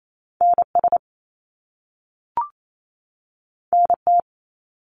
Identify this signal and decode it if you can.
Morse code, heard as a keyed tone, about 28 wpm, 700 Hz. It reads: NS NT
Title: Canadian Provinces and Territories 28wpm